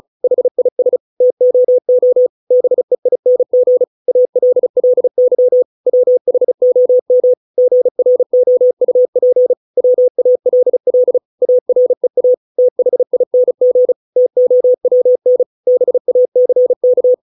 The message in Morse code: HIS TOO BEING ALLY WHOM GROUP WALL AREA THING TOWN BACK

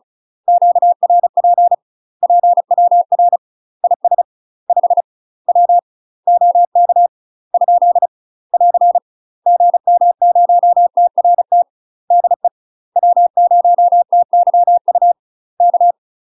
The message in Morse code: QRP PWR IS 5 W OK ? <AR> GM0TRT DE W0TYU K

